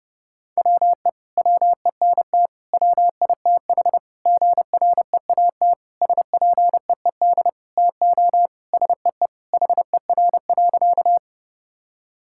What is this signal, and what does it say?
Morse code, heard as a keyed tone, about 30 wpm, 700 Hz. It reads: WE WENT WITH GREAT SPEED TO SEE HER.
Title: Sentences from Top 1000 Words - Review Part 5 - 30wpm